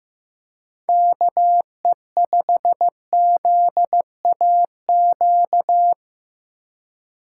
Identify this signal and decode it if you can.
Morse code, heard as a keyed tone, about 15 wpm, 700 Hz. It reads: KE5ZAQ
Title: USA Call Signs - 6x Character Spacing - 15wpm